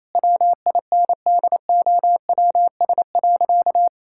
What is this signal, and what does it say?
Morse code, heard as a keyed tone, about 28 wpm, 700 Hz. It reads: WINDOWS.
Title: Pangram Challenge v2 - 28wpm